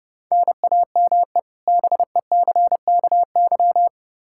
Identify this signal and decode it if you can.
Morse code, heard as a keyed tone, about 31 wpm, 700 Hz. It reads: NAME BECKY
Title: Practice QSO - Exchange 1 - Beginning - 30wpm